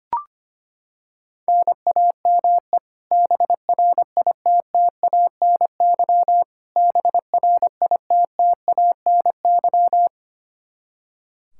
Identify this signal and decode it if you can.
Morse code, heard as a keyed tone, about 25 wpm, 700 Hz. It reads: NAME BRITTANY BRITTANY
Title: Practice QSO - Exchange 1 - Beginning - 25wpm